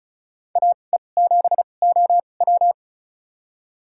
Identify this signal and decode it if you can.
Morse code, heard as a keyed tone, about 36 wpm, spaced as for 25 wpm, 700 Hz. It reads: AE7OW
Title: USA Call Signs - 5 Characters - 2x Character Spacing - 35wpm